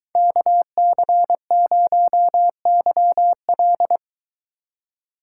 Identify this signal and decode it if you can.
Morse code, heard as a keyed tone, about 23 wpm, 700 Hz. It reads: KC0YL